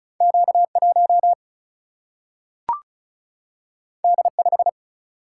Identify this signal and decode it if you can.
Morse code, heard as a keyed tone, about 36 wpm, 700 Hz. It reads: Q 1 D 5
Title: Single-Letter Single-Number 35wpm